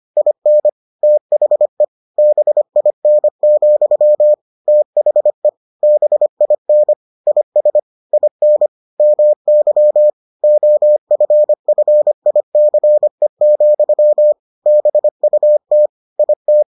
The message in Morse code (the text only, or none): IN THE BIN, THE BIN IS IN MY OFFICE, BUT IT